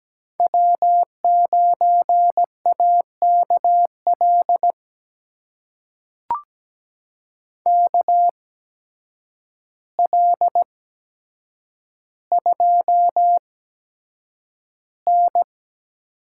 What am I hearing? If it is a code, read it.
Morse code, heard as a keyed tone, about 17 wpm, spaced as for 8 wpm, 700 Hz. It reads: W9AKL KL2N
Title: USA Call Signs - 5 Characters - 8x Character Spacing - 17wpm